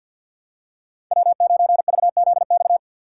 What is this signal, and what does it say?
Morse code, heard as a keyed tone, about 47 wpm, 700 Hz. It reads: W9VZX
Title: USA Call Signs - 6x Character Spacing - 50wpm